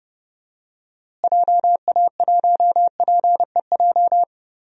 Morse code, heard as a keyed tone, about 30 wpm, 700 Hz. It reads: JA1PEJ
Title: DXCC-100 Call Signs - 6 Characters - Repeated 3x - 30wpm